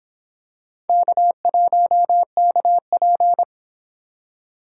Morse code, heard as a keyed tone, about 26 wpm, 700 Hz. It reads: K1KP